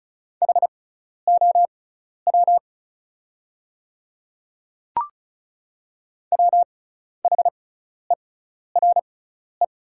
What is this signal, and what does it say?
Morse code, heard as a keyed tone, about 36 wpm, spaced as for 12 wpm, 690 Hz. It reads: HOW WHERE
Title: 6x Character Spacing - Top 300 Words - 35wpm